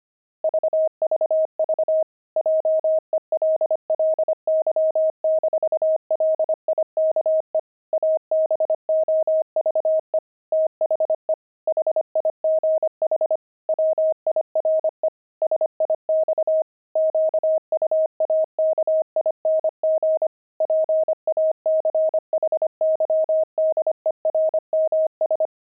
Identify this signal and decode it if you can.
Morse code, heard as a keyed tone, about 25 wpm, 630 Hz. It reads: VVV JELLY-LIKE ABOVE THE HIGH WIRE SIX QUAKING PACHYDERMS
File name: pangram-fs-25-cs-25-jelly